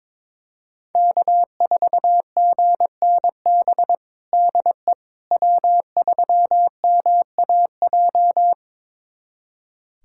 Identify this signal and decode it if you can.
Morse code, heard as a keyed tone, about 22 wpm, 700 Hz. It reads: K4GNB DE W3MAJ
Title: USA Callsign DE Callsign - 5 Characters - 22wpm